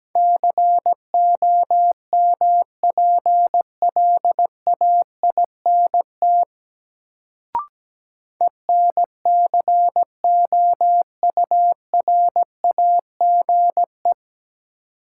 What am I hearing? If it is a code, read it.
Morse code, heard as a keyed tone, about 17 wpm, 700 Hz. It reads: COMPLAINT ENCOURAGE
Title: Sentences from Top 1400 Words - New Word Drill Alt - 17wpm